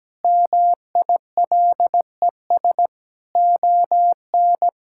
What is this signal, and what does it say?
Morse code, heard as a keyed tone, about 17 wpm, 700 Hz. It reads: MILES ON
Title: Pangram Challenge v2 - 17wpm